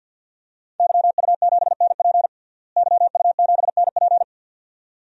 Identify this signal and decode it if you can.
Morse code, heard as a keyed tone, about 51 wpm, 700 Hz. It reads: YU7NP YU7NP
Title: DXCC-100 Call Signs - 5 Characters - Repeated 2x - 50wpm